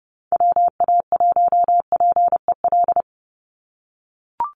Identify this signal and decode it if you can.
Morse code, heard as a keyed tone, about 30 wpm, 700 Hz. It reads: WA1PEL
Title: USA Call Signs 30wpm